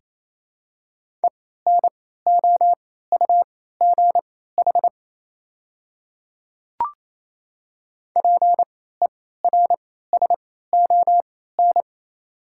Morse code, heard as a keyed tone, about 28 wpm, spaced as for 16 wpm, 700 Hz. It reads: ENOUGH PERSON
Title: 3x Character Spacing - Top 500 Words - 28wpm